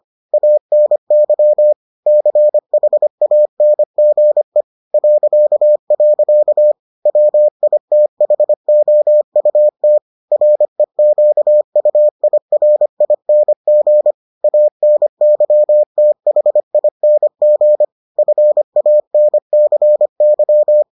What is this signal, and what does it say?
Morse code, heard as a keyed tone, about 25 wpm, 600 Hz. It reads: ANY CHANGE .. WITHOUT REQUIRING ANYTHING FANCY